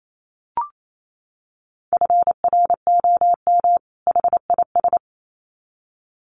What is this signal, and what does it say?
Morse code, heard as a keyed tone, about 28 wpm, 700 Hz. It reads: FROM HIS
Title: Sets of 2 Words 28wpm